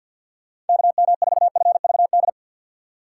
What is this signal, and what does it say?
Morse code, heard as a keyed tone, about 54 wpm, 700 Hz. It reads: KM4FVD
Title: USA Call Signs - 6 Characters - 6x Character Spacing - 50wpm